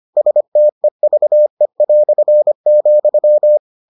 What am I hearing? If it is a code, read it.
Morse code, heard as a keyed tone, about 25 wpm, 600 Hz. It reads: STEVE",